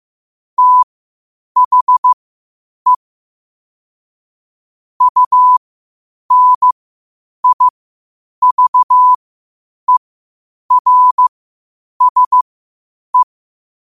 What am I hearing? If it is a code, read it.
Morse code, heard as a keyed tone, about 15 wpm, spaced as for 8 wpm, 1000 Hz. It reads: THE UNIVERSE
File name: qotdfiles / 8wpm